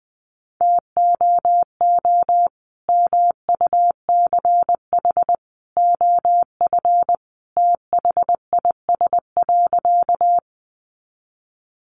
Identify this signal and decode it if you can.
Morse code, heard as a keyed tone, about 20 wpm, 700 Hz. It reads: TOO MUCH OF THIS.
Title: Sentences from Top 200 Words 20wpm